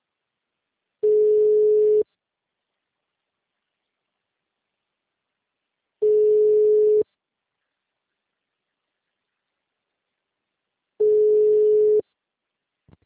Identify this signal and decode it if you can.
Ringback tone (Europe, 425 Hz).
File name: out- -2021- - - .52580